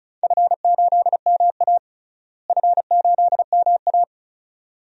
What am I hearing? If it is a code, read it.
Morse code, heard as a keyed tone, about 34 wpm, 700 Hz. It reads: F8MA F8MA
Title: DXCC-100 Call Signs - 4 Characters - Repeated 2x - 35wpm